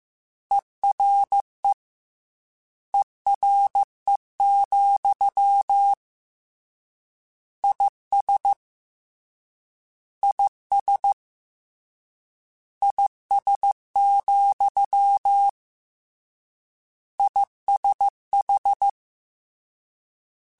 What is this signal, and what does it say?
Morse code, heard as a keyed tone, about 15 wpm, 800 Hz. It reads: ERE ERE, IS IS IS, ISH